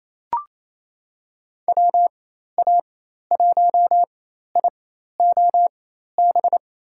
Morse code, heard as a keyed tone, about 28 wpm, spaced as for 13 wpm, 700 Hz. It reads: WA1IOB
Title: USA Call Signs - 6 Characters - 4x Character Spacing - 28wpm